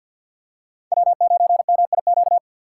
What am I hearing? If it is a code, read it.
Morse code, heard as a keyed tone, about 52 wpm, 700 Hz. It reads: W9MIQ
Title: USA Call Signs - 5 Characters - 6x Character Spacing - 50wpm